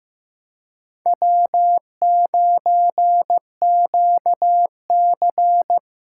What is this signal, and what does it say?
Morse code, heard as a keyed tone, about 16 wpm, 700 Hz. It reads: W9QC
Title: USA Callsign DE Callsign - 4 Characters - 15wpm